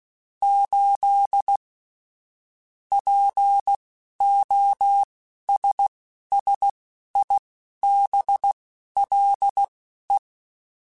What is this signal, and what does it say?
Morse code, heard as a keyed tone, about 16 wpm, spaced as for 11 wpm, 800 Hz. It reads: 8 POSSIBLE
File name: Technical Subjects / Technical Subjects-mp3s / AND Gate @10